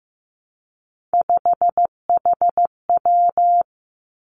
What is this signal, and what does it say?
Morse code, heard as a keyed tone, about 15 wpm, 700 Hz. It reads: 5HW
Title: Three Challenging Letter-Number Combinations 15wpm